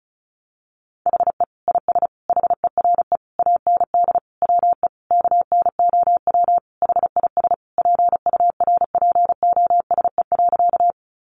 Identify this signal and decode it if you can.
Morse code, heard as a keyed tone, about 34 wpm, 700 Hz. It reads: HE IS HERE AND WE KNOW HIS PURPOSE.
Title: Sentences from Top 800 Words 35wpm